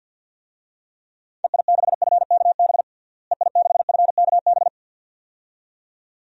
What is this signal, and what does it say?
Morse code, heard as a keyed tone, about 47 wpm, 700 Hz. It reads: EI6FKB EI6FKB
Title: DXCC-100 Call Signs - 6 Characters - Repeated 2x - 50wpm